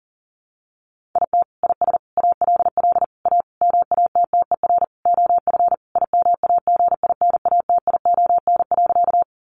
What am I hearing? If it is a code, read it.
Morse code, heard as a keyed tone, about 40 wpm, 700 Hz. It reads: IT IS ALL A MATTER OF IMAGINATION.
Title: Sentences from Top 1300 Words 40wpm